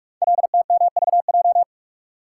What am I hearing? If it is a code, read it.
Morse code, heard as a keyed tone, about 44 wpm, 690 Hz. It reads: LTMVJ
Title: Code Groups - 5 Letters - 45wpm